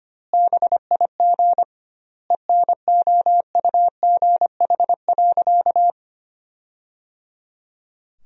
Morse code, heard as a keyed tone, about 25 wpm, 700 Hz. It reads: BIG ENOUGH.
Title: 2x Word-Spacing - Sentences from Top 400 Words - 25wpm